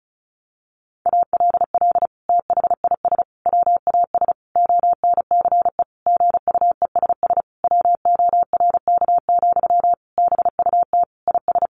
Morse code, heard as a keyed tone, about 34 wpm, 700 Hz. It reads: ALL THIS WAS ONCE GUESS WORK, BUT IS
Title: Sentences from Top 1500 Words 35wpm